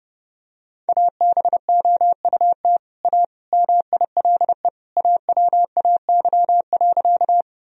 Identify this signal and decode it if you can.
Morse code, heard as a keyed tone, about 32 wpm, 700 Hz. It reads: ABOUT A MILE AWAY.
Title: Sentences from Top 1000 Words - Review Part 1 - 30wpm